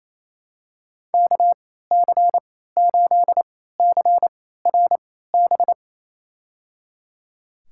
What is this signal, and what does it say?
Morse code, heard as a keyed tone, about 27 wpm, spaced as for 16 wpm, 700 Hz. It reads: KC8CRB
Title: USA Call Signs - 6 Characters - 3x Character Spacing - 28wpm